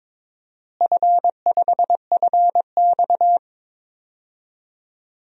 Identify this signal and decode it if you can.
Morse code, heard as a keyed tone, about 22 wpm, 700 Hz. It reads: F5FX
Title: DXCC-100 Call Signs - 4 Characters - Repeated 3x - 22wpm